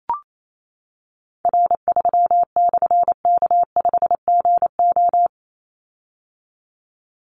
Morse code, heard as a keyed tone, about 28 wpm, 700 Hz. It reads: R3/K5GO
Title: Call Signs - 28wpm